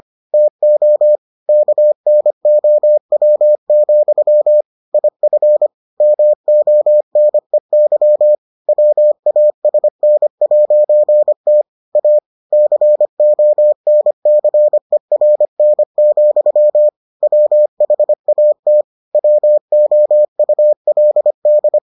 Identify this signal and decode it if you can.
Morse code, heard as a keyed tone, about 25 wpm, 600 Hz. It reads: TO KNOW, IF MONEY WASN'T A CONCERN, WHAT WOULD